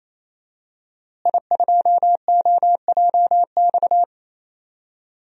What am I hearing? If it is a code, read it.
Morse code, heard as a keyed tone, about 28 wpm, 700 Hz. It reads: I2OJX
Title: DXCC-100 Call Signs - 5 Characters - Repeated 3x - 28wpm